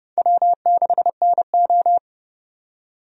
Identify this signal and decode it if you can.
Morse code, heard as a keyed tone, about 30 wpm, 700 Hz. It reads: W6NO
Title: USA Call Signs - 4 Characters - 30wpm